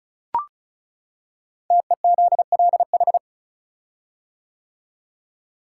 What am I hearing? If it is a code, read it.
Morse code, heard as a keyed tone, about 36 wpm, 700 Hz. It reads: TEZLH
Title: Code Groups - 5 Letters - 35wpm